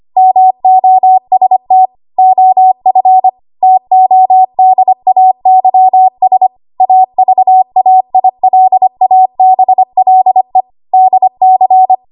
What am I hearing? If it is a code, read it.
Morse code, heard as a keyed tone, about 25 wpm, 750 Hz. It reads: MOST OF TODAYS AVAILABLE DC
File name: Morse / 25WPM